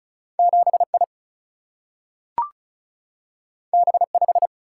Morse code, heard as a keyed tone, about 36 wpm, 700 Hz. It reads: 7 I B 5
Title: Single-Letter Single-Number 35wpm